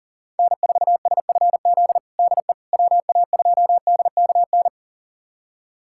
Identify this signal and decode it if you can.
Morse code, heard as a keyed tone, about 40 wpm, 700 Hz. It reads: N4SFZ DE WA2DKN
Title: USA Callsign DE Callsign 40wpm